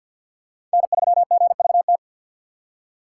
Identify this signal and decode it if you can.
Morse code, heard as a keyed tone, about 49 wpm, 700 Hz. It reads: N3GVT
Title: USA Call Signs - 5 Characters - 4x Character Spacing - 50wpm